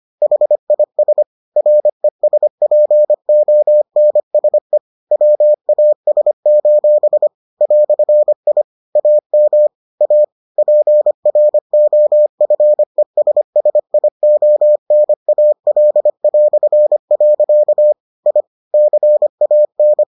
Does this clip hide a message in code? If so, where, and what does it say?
Yes: Morse code from 0.2 to 20.0 s, heard as a keyed tone, about 25 wpm, 600 Hz. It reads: HIS RESPONSE WAS: "I AM A PROFESSIONAL". I CAN